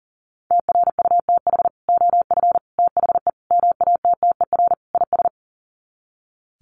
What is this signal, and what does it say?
Morse code, heard as a keyed tone, about 40 wpm, 700 Hz. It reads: TRUTH OF THE MATTER IS
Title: Sets of 5 Words Encore2 40wpm